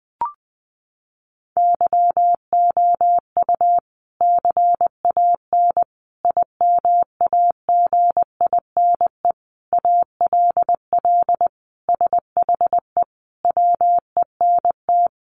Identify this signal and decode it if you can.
Morse code, heard as a keyed tone, about 20 wpm, 700 Hz. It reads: YOU CAN IMAGINE ALL SHE WENT
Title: Sentences from Top 1400 Words 20wpm